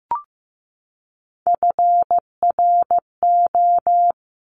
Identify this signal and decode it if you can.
Morse code, heard as a keyed tone, about 15 wpm, 700 Hz. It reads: FRO
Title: Top Three Letter Combinations in Words 15wpm